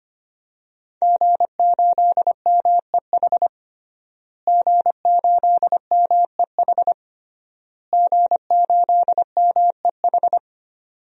Morse code, heard as a keyed tone, about 25 wpm, 700 Hz. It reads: G8MEH G8MEH G8MEH
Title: DXCC-100 Call Signs - 5 Characters - Repeated 3x - 25wpm